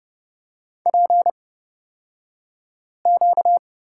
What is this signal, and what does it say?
Morse code, heard as a keyed tone, about 30 wpm, spaced as for 5 wpm, 700 Hz. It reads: PQ
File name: Lesson 38d - Letter q - Two Characters 30wpm